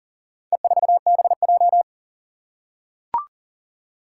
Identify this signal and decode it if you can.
Morse code, heard as a keyed tone, about 40 wpm, 690 Hz. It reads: E4BJ
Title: Four Challenging Letter-Number Combinations 40wpm